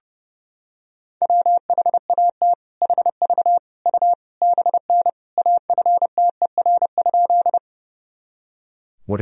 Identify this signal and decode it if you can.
Morse code, heard as a keyed tone, about 32 wpm, 700 Hz. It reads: WHAT HV U BN AFTER?
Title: Sentences from Top 200 Words and Abbreviations 30wpm